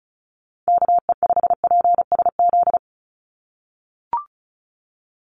Morse code, heard as a keyed tone, about 34 wpm, 700 Hz. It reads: KE5PSZ
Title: USA Call Signs 35wpm